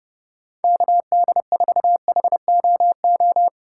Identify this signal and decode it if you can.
Morse code, heard as a keyed tone, about 32 wpm, 700 Hz. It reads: KD4HOO
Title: USA Call Signs - 6 Characters - 4x Character Spacing - 30wpm